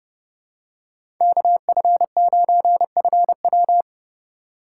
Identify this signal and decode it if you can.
Morse code, heard as a keyed tone, about 29 wpm, 700 Hz. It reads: KF9FW
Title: USA Call Signs - 2x Character Spacing - 30wpm